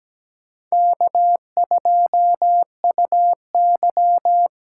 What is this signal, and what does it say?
Morse code, heard as a keyed tone, about 20 wpm, 700 Hz. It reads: K2UY
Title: Speed Racing - USA 1x2 Call Signs 17wpm